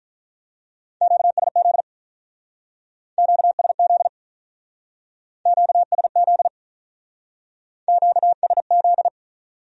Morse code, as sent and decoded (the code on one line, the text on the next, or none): --.- ... --.. / --.- ... --.. / --.- ... --.. / --.- ... --..
QSZ QSZ QSZ QSZ